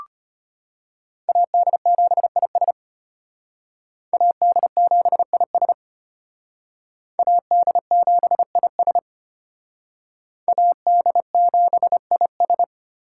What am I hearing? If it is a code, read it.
Morse code, heard as a keyed tone, about 32 wpm, 700 Hz. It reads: AD7IS AD7IS AD7IS AD7IS